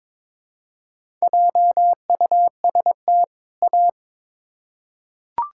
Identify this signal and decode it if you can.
Morse code, heard as a keyed tone, about 22 wpm, 700 Hz. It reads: JUST A
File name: Build A Phrase - 2-Word Phrases - 22wpm